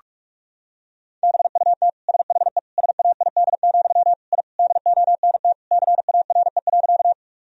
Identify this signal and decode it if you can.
Morse code, heard as a keyed tone, about 41 wpm, 700 Hz. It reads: BUT SHE SAID, I DONT CARE.
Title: Sentences from Top 1000 Words - Review Part 1 - 45wpm